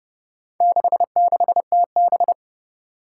Morse code, heard as a keyed tone, about 29 wpm, 700 Hz. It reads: 66TB
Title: Four Challenging Letter-Number Combinations 30wpm